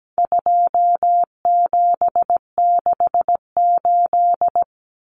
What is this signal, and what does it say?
Morse code, heard as a keyed tone, about 17 wpm, 700 Hz. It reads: 2768
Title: Common Cut Numbers - 4-Digit 17wpm